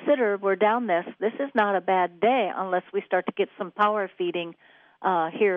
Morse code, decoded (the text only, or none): none